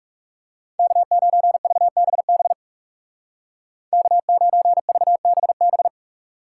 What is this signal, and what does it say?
Morse code, heard as a keyed tone, about 40 wpm, 700 Hz. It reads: K9VBB K9VBB